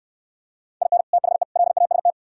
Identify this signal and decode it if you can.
Morse code, heard as a keyed tone, about 66 wpm, 700 Hz. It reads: IN THE LAST